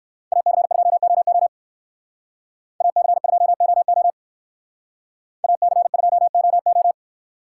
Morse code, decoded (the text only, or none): AC2OO AC2OO AC2OO